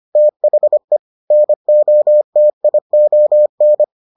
Morse code, heard as a keyed tone, about 25 wpm, 600 Hz. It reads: THE NOTION